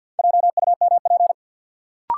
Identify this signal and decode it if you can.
Morse code, heard as a keyed tone, about 52 wpm, 700 Hz. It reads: JUMP
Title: Sentences from Top 1300 Words - New Word Drill - 50wpm